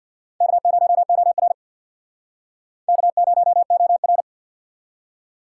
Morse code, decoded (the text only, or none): K0OR K0OR